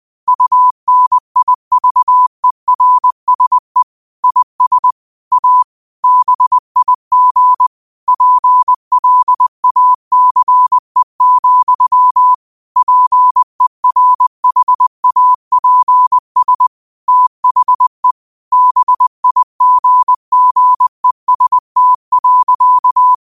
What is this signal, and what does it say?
Morse code, heard as a keyed tone, about 20 wpm, 1000 Hz. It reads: UNIVERSE IS A BIG PLACE, PERHAPS THE BIGGEST.
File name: 20wpm